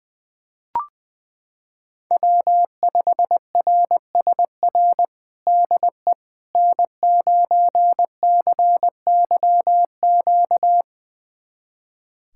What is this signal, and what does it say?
Morse code, heard as a keyed tone, about 20 wpm, 700 Hz. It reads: W5RSR DE N9CYQ
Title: USA Callsign DE Callsign - 5 Characters - 20wpm